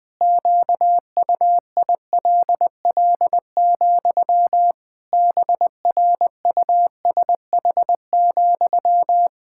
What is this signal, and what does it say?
Morse code, heard as a keyed tone, about 20 wpm, 700 Hz. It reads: QUILL, BRUSH,
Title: Pangram Challenge v2 - 20wpm